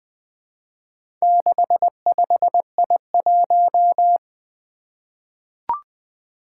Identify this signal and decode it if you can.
Morse code, heard as a keyed tone, about 20 wpm, 700 Hz. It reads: 65I1
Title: Four Letter-Number Combinations - 20wpm